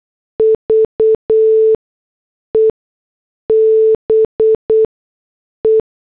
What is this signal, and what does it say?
Morse code, heard as a keyed tone, about 8 wpm, spaced as for 6 wpm, 440 Hz. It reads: VE BE